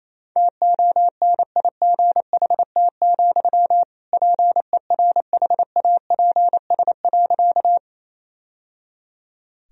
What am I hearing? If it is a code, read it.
Morse code, heard as a keyed tone, about 28 wpm, 700 Hz. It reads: TONIGHT, PERHAPS.
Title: Sentences from Top 1000 Words - Review Part 3 - 28wpm